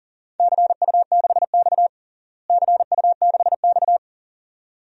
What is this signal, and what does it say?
Morse code, heard as a keyed tone, about 40 wpm, 700 Hz. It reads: CU6X CU6X
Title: DXCC-100 Call Signs - 4 Characters - Repeated 2x - 40wpm